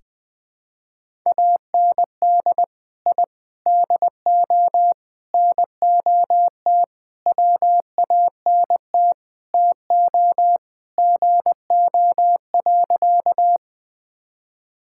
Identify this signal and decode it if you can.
Morse code, heard as a keyed tone, about 20 wpm, 700 Hz. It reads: AND I DO NOT WANT TO GO.
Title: Sentences with 5-Word Phrases - Spoken Beginning - 20wpm